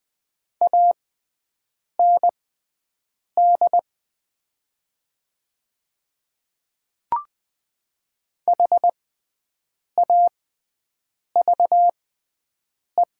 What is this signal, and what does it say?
Morse code, heard as a keyed tone, about 20 wpm, spaced as for 7 wpm, 700 Hz. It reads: AND HAVE
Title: 6x Character Spacing - Top 100 Words - 20wpm